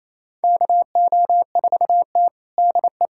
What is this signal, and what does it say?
Morse code, heard as a keyed tone, about 28 wpm, 700 Hz. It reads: KO4T DE
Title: USA Callsign DE Callsign - 4 Characters - 28wpm